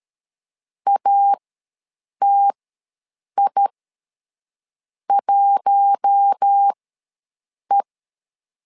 Morse code, heard as a keyed tone, about 13 wpm, spaced as for 6 wpm, 800 Hz. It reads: ATI1E
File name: LEZIONI / 4 Q1 ws 12wpm 2129